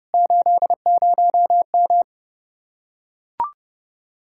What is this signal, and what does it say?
Morse code, heard as a keyed tone, about 29 wpm, 700 Hz. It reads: 80M